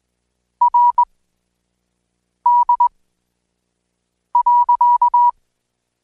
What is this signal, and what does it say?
Morse code, heard as a keyed tone, about 19 wpm, spaced as for 5 wpm, 970 Hz. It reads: RD.
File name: Code Quick